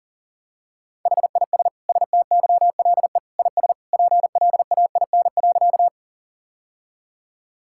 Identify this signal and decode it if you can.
Morse code, heard as a keyed tone, about 37 wpm, 700 Hz. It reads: HIS STYLE IS PLAIN.
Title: Sentences from Top 1000 Words - Review Part 5 - 40wpm